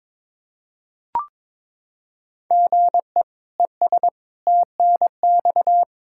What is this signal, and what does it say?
Morse code, heard as a keyed tone, about 21 wpm, 700 Hz. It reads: GE ES TNX
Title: Practice QSO - Exchange 1 - Beginning - 22wpm